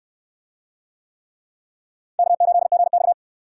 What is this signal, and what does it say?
Morse code, heard as a keyed tone, about 63 wpm, 700 Hz. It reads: K8GX